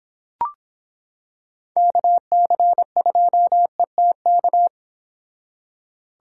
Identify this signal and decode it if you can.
Morse code, heard as a keyed tone, about 25 wpm, 700 Hz. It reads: KC2ETK